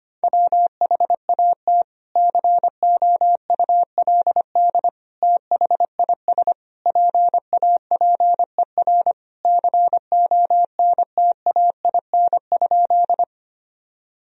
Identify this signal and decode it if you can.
Morse code, heard as a keyed tone, about 25 wpm, 700 Hz. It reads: WHAT COULD THIS PAPER CONTAIN?
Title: Sentences from Top 1000 Words - Review Part 2 - 25wpm